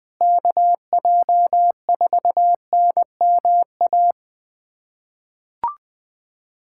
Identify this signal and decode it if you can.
Morse code, heard as a keyed tone, about 20 wpm, 700 Hz. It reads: KJ4NMA
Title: USA Callsign DE Callsign - 6 Characters - 20wpm